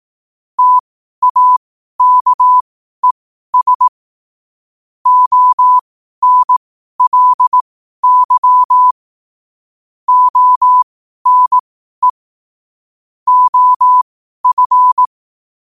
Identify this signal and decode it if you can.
Morse code, heard as a keyed tone, about 18 wpm, spaced as for 12 wpm, 1000 Hz. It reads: TAKES ONLY ONE OF
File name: qotdfiles / 12wpm